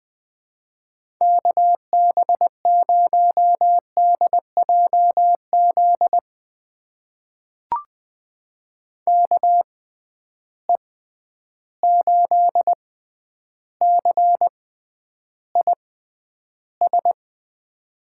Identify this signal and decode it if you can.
Morse code, heard as a keyed tone, about 21 wpm, spaced as for 10 wpm, 700 Hz. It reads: KB0DJZ KE8CIS
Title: USA Call Signs - 6 Characters - 6x Character Spacing - 20wpm